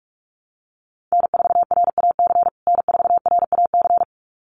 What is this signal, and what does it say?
Morse code, heard as a keyed tone, about 44 wpm, 700 Hz. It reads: N4RAC N4RAC
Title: USA Call Signs Repeated 2x 45wpm